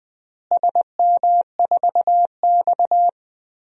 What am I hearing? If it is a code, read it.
Morse code, heard as a keyed tone, about 20 wpm, 700 Hz. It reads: SM4X
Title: DXCC-100 Call Signs - 4 Characters - Repeated 3x - 20wpm